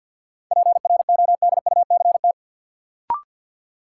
Morse code, heard as a keyed tone, about 53 wpm, 700 Hz. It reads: PRODUCT
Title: Sentences from Top 1000 Words - Review Part 2 - New Word Drill - 50wpm